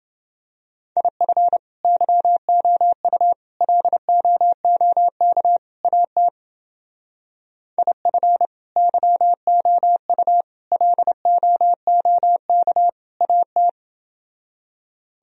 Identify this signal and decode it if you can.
Morse code, heard as a keyed tone, about 29 wpm, 700 Hz. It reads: IF YOU LOOK AT IF YOU LOOK AT